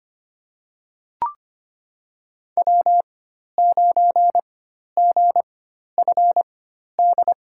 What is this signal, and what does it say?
Morse code, heard as a keyed tone, about 25 wpm, spaced as for 12 wpm, 700 Hz. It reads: W9GFD
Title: USA Call Signs - 5 Characters - 4x Character Spacing - 25wpm